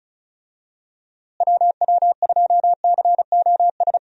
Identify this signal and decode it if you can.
Morse code, heard as a keyed tone, about 34 wpm, 700 Hz. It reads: WW2COS
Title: USA Call Signs - 6 Characters - Repeated 3x - 35wpm